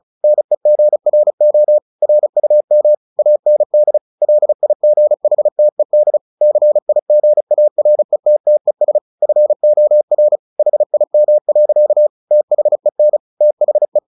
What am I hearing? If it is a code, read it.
Morse code, heard as a keyed tone, about 35 wpm, 600 Hz. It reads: NEGRO RUM AND LIGHTED CIGARETTES FOR HIM. THEN THE
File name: the old man and the sea 2 35wpm